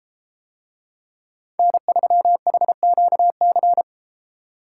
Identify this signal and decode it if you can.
Morse code, heard as a keyed tone, about 32 wpm, 700 Hz. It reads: N3HQC